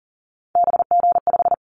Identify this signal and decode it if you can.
Morse code, heard as a keyed tone, about 40 wpm, 700 Hz. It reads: BG5
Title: Callsign Prefixes for Top 100 DXCC - 40wpm